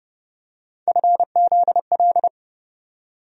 Morse code, heard as a keyed tone, about 30 wpm, 700 Hz. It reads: FZL
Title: USA Call Sign Postfixes - 3 Letters 30wpm